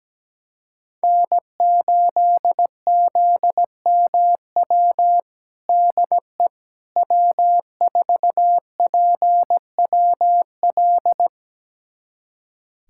Morse code, heard as a keyed tone, about 17 wpm, 700 Hz. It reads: N8ZMW DE W4PWL
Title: USA Callsign DE Callsign - 5 Characters - 17wpm